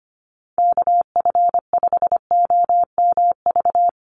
Morse code, heard as a keyed tone, about 25 wpm, 700 Hz. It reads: KF5OMV
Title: USA Call Signs Repeated 2x 25wpm